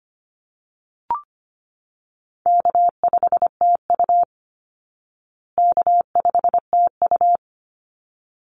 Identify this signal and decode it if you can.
Morse code, heard as a keyed tone, about 25 wpm, 700 Hz. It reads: K5TU K5TU
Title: Call Signs Repeated 3x - 25wpm